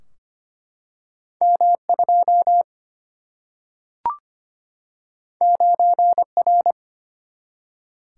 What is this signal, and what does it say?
Morse code, heard as a keyed tone, about 25 wpm, 700 Hz. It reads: M2 9R
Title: Single-Letter Single-Number 25wpm